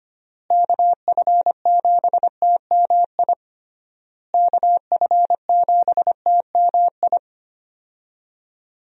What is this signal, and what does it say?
Morse code, heard as a keyed tone, about 25 wpm, 700 Hz. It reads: KF7TMI KF7TMI
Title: USA Call Signs - 6 Characters - Repeated 3x - 25wpm